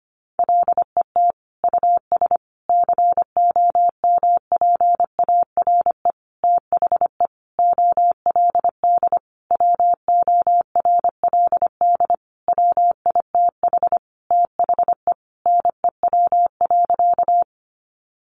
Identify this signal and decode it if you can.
Morse code, heard as a keyed tone, about 25 wpm, 700 Hz. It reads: LET US COMPARE THE OLD WORLD WITH THE NEW.
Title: Sentences from Top 1300 Words 25wpm